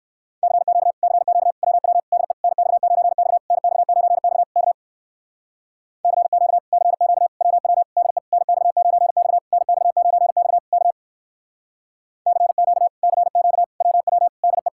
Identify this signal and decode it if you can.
Morse code, heard as a keyed tone, about 54 wpm, 700 Hz. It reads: CQ CQ WW DE NX9X NX9X K CQ CQ WW DE NX9X NX9X K CQ CQ WW DE
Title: Speed-Racing Common QSO Phrases - 45wpm